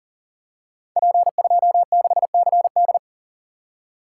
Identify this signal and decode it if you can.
Morse code, heard as a keyed tone, about 42 wpm, 690 Hz. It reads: P26CD
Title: Code Groups - 5 Letter:Numbers - 40wpm